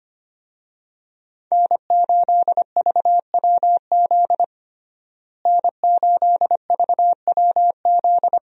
Morse code, heard as a keyed tone, about 25 wpm, 700 Hz. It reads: N8VWZ N8VWZ
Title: USA Call Signs - 5 Characters - Repeated 3x - 25wpm